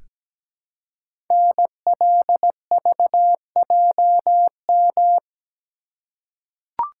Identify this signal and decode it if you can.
Morse code, heard as a keyed tone, about 17 wpm, 700 Hz. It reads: NLVJM
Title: Code Groups - 5 Letters - 17wpm